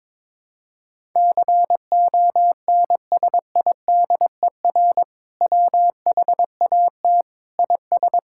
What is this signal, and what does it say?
Morse code, heard as a keyed tone, about 22 wpm, 700 Hz. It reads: CONSIDER WHAT IS
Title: Sentences from Top 1000 Words - Review Part 5 - 22wpm